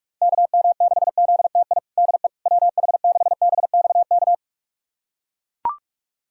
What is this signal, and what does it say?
Morse code, heard as a keyed tone, about 41 wpm, 700 Hz. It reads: KM6ZTI DE WH6BXK
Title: USA Callsign DE Callsign - 6 Characters - 45wpm